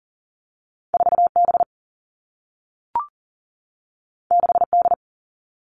Morse code, heard as a keyed tone, about 40 wpm, 700 Hz. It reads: 4B 6D
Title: All Cut Numbers - 2-Digit 40wpm